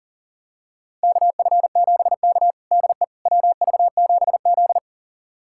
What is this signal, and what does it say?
Morse code, heard as a keyed tone, about 40 wpm, 700 Hz. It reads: KF7K DE WV7Z
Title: USA Callsign DE Callsign - 4 Characters - 40wpm